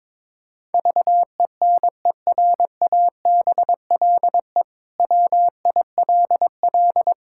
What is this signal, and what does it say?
Morse code, heard as a keyed tone, about 22 wpm, 700 Hz. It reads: VENERABLE WILL
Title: Pangram Challenge v2 - 22wpm